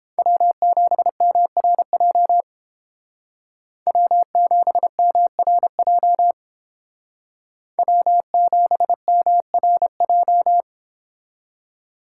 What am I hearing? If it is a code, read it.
Morse code, heard as a keyed tone, about 29 wpm, 700 Hz. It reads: W7MRJ W7MRJ W7MRJ